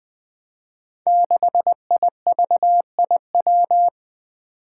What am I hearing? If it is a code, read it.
Morse code, heard as a keyed tone, about 20 wpm, 700 Hz. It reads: 6IVIW
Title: Five Challenging Letter-Number Combinations 20wpm